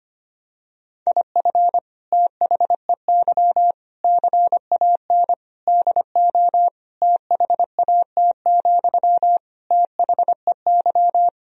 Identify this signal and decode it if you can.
Morse code, heard as a keyed tone, about 25 wpm, 700 Hz. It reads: IF THEY CAN DO THAT, THEY
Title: Sentences from Top 1000 Words - Review Part 4 - 25wpm